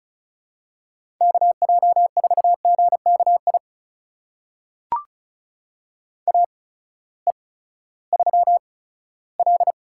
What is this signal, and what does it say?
Morse code, heard as a keyed tone, about 34 wpm, spaced as for 17 wpm, 700 Hz. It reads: KJ4GKI AE3L
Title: USA Call Signs - 8x Character Spacing - 35wpm